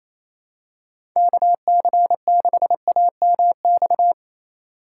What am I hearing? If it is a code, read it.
Morse code, heard as a keyed tone, about 28 wpm, 700 Hz. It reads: KC6AMX
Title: USA Call Signs - 6 Characters - 3x Character Spacing - 28wpm